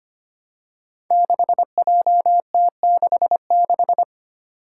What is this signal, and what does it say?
Morse code, heard as a keyed tone, about 25 wpm, 700 Hz. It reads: 6JT66
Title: Five Challenging Letter-Number Combinations 25wpm